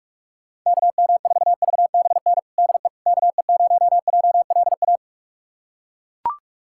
Morse code, heard as a keyed tone, about 41 wpm, 700 Hz. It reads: KM4VBN DE KE0JLA
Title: USA Callsign DE Callsign - 6 Characters - 45wpm